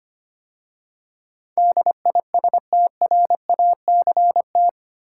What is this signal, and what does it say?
Morse code, heard as a keyed tone, about 25 wpm, 690 Hz. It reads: DISTRACT
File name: 8 Letter Words - First 3 Letters Given - 25wpm